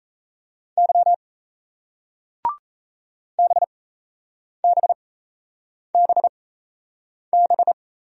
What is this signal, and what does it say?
Morse code, heard as a keyed tone, about 37 wpm, spaced as for 8 wpm, 700 Hz. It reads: Y BBBB